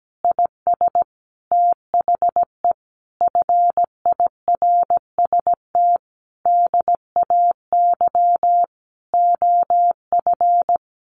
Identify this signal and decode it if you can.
Morse code, heard as a keyed tone, about 17 wpm, 700 Hz. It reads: IS THE FIRST DAY OF
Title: Additional Easy Phrases 17wpm